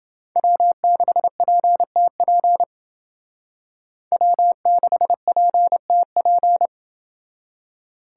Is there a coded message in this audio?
Yes: Morse code from 0.4 to 6.7 s, heard as a keyed tone, about 28 wpm, 700 Hz. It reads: W6PTP W6PTP